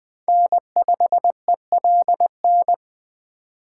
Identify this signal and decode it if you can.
Morse code, heard as a keyed tone, about 20 wpm, 700 Hz. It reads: N5ELN
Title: Speed Racing - USA 1x3 Call Signs 17wpm